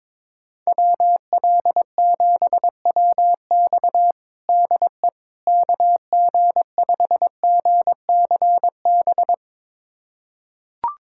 Morse code, heard as a keyed tone, about 22 wpm, 700 Hz. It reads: WL7WX DE KG5GCB
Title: USA Callsign DE Callsign 22wpm